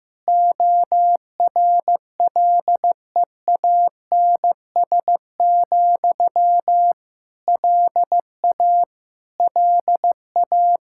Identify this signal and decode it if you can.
Morse code, heard as a keyed tone, about 15 wpm, 700 Hz. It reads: ORLEANS, LA LA
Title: Practice QSO - Exchange 1 - Beginning - 15wpm